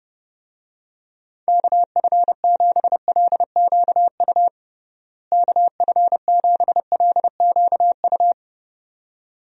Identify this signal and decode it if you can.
Morse code, heard as a keyed tone, about 29 wpm, 700 Hz. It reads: KF7LQU KF7LQU
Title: USA Call Signs - 6 Characters - Repeated 2x - 30wpm